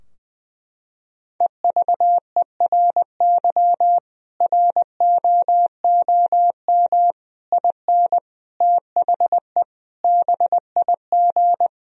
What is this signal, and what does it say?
Morse code, heard as a keyed tone, about 20 wpm, 700 Hz. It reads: EVERY ROOM IN THE BIG
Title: Sentences from Top 1000 Words - Review Part 3 - 20wpm